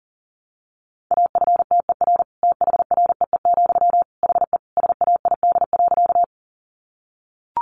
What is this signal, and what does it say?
Morse code, heard as a keyed tone, about 40 wpm, 700 Hz. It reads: AFTER THREE, HE SAID.
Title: Sentences from Top 200 Words 40wpm